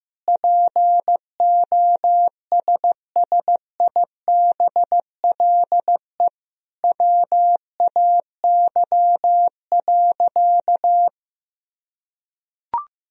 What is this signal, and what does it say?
Morse code, heard as a keyed tone, about 15 wpm, 700 Hz. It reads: POSSIBLE WAY.
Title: Sentences from Top 500 Words and Abbreviations 15wpm